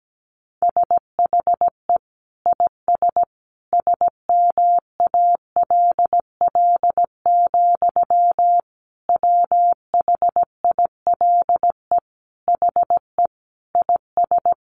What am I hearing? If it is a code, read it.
Morse code, heard as a keyed tone, about 17 wpm, 700 Hz. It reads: SHE IS SMALL, WHILE HE IS
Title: Sentences from Top 1300 Words 17wpm